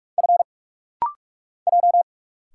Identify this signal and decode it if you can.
Morse code, heard as a keyed tone, about 46 wpm, 700 Hz. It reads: F J
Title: Single Letter-Number - Rapid-Fire 45wpm